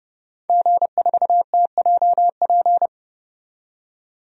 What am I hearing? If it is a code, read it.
Morse code, heard as a keyed tone, about 32 wpm, 700 Hz. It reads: G4TJP
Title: DXCC-100 Call Signs - 5 Characters - 30wpm